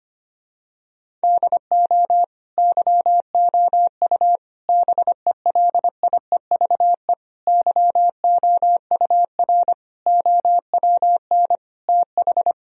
Morse code, heard as a keyed tone, about 25 wpm, 700 Hz. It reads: DO YOU BELIEVE YOUR OWN TH
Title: Sentences from Top 1000 Words - Review Part 3 - 25wpm